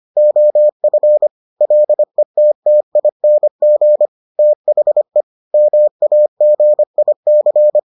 Morse code, heard as a keyed tone, about 25 wpm, 600 Hz. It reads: OF LETTING THE MAGIC